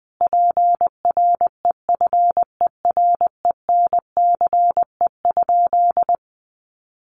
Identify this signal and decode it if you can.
Morse code, heard as a keyed tone, about 20 wpm, 700 Hz. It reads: PREFERENCE?
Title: Sentences from Top 1500 Words 20wpm